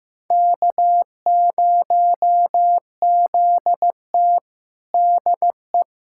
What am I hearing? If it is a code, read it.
Morse code, heard as a keyed tone, about 15 wpm, 700 Hz. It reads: K0ZT DE
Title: USA Callsign DE Callsign - 4 Characters - 15wpm